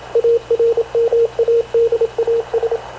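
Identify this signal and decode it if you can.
Morse code, heard as a keyed tone, about 27 wpm, 460 Hz, with radio noise. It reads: ARMADAS
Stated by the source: Havana Radio / CLA sending Spanish press (PX) in Morse, recorded by Radio Officer Paul Dumesnil onboard CNAV QUEST/CZDO.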